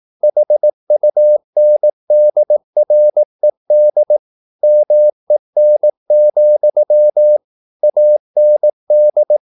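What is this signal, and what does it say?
Morse code, heard as a keyed tone, about 18 wpm, 600 Hz. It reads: HUNDRED MEN, AND
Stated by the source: Morse practice at 18 wpm, English text, from CW med Gnister.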